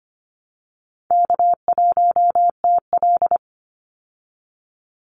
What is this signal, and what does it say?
Morse code, heard as a keyed tone, about 25 wpm, 700 Hz. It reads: K1TL
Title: Easy Call Signs Repeated 2x - 25wpm